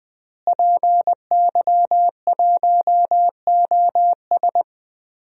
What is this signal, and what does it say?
Morse code, heard as a keyed tone, about 21 wpm, 700 Hz. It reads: PY1OS
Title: DXCC-100 Call Signs - 5 Characters - Repeated 3x - 20wpm